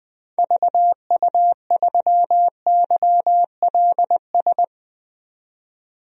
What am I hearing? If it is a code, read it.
Morse code, heard as a keyed tone, about 20 wpm, 700 Hz. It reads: VU3YLS
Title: DXCC-100 Call Signs - 6 Characters - 20wpm